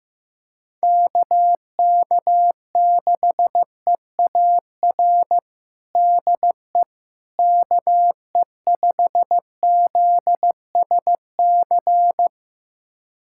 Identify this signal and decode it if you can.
Morse code, heard as a keyed tone, about 16 wpm, 700 Hz. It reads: KK6EAR DE KE5ZSC
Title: USA Callsign DE Callsign 15wpm